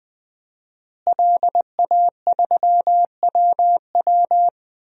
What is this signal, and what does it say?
Morse code, heard as a keyed tone, about 20 wpm, 700 Hz. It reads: LA3WW
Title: DXCC-100 Call Signs - 5 Characters - Repeated 2x - 20wpm